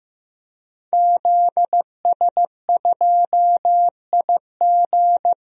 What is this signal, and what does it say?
Morse code, heard as a keyed tone, about 16 wpm, 700 Hz. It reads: ZS2IG
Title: DXCC-100 Call Signs - 5 Characters - Repeated 3x - 15wpm